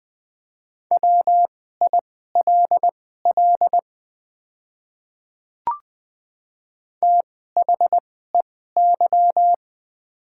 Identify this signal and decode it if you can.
Morse code, heard as a keyed tone, about 21 wpm, spaced as for 15 wpm, 690 Hz. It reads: WILL THEY
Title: 2x Character Spacing - Top 300 Words - 20wpm